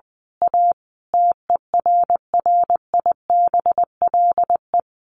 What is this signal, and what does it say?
Morse code, heard as a keyed tone, about 20 wpm, 700 Hz. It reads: A TERRIBLE
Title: A Sound Of Thunder 20wpm